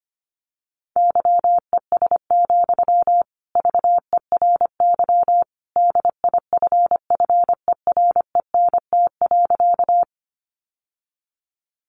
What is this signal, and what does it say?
Morse code, heard as a keyed tone, about 25 wpm, 700 Hz. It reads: YES, VERY DIFFERENT.
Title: Sentences from Top 600 Words 25wpm